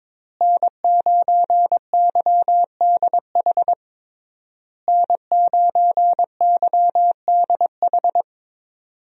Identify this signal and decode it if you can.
Morse code, heard as a keyed tone, about 22 wpm, 700 Hz. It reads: N9YDH N9YDH
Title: USA Call Signs - 5 Characters - Repeated 3x - 22wpm